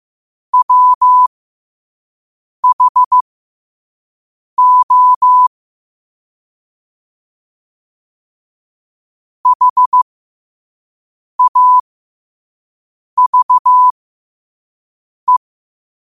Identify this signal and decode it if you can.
Morse code, heard as a keyed tone, about 15 wpm, spaced as for 5 wpm, 1000 Hz. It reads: WHO HAVE
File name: qotdfiles / 5wpm